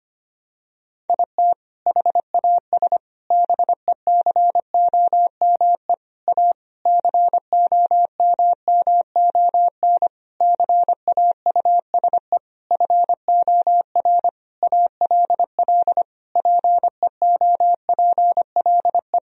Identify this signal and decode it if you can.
Morse code, heard as a keyed tone, about 25 wpm, 700 Hz. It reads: IT HAS BECOME A COMMON CAUSE FOR ALL PEOPLE
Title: Sentences from Top 1000 Words - Review Part 3 - 25wpm